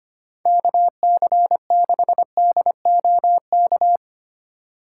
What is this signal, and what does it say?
Morse code, heard as a keyed tone, about 25 wpm, 700 Hz. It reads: KC6DOK
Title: USA Call Signs - 6 Characters - 4x Character Spacing - 25wpm